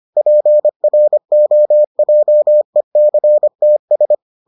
Morse code, heard as a keyed tone, about 25 wpm, 600 Hz. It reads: PROJECTS